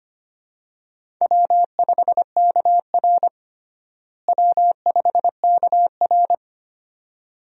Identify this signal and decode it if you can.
Morse code, heard as a keyed tone, about 25 wpm, 700 Hz. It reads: W5KR W5KR
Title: USA Call Signs - 4 Characters - Repeated 2x - 25wpm